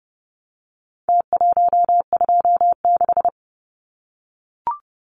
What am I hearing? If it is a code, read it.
Morse code, heard as a keyed tone, about 30 wpm, 700 Hz. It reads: T126
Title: Numbers - Contest Style - 4-Digit 30wpm